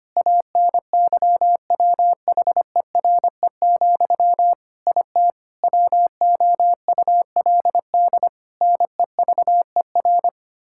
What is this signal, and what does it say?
Morse code, heard as a keyed tone, about 25 wpm, 700 Hz. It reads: ANYWHERE, IT WOULD NEVER
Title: Sentences from Top 1600 Words 25wpm